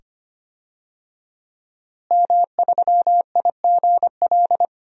Morse code, heard as a keyed tone, about 25 wpm, 700 Hz. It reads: M3IGL
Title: DXCC-100 Call Signs - 5 Characters - Repeated 2x - 25wpm